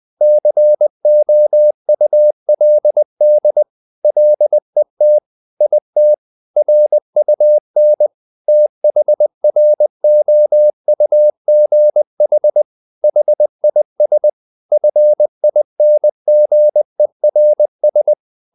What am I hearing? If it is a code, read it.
Morse code, heard as a keyed tone, about 20 wpm, 600 Hz. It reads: COULD LET IT RUN THROUGH HIS FINGERS
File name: the old man and the sea / the old man and the sea part 1 20wpm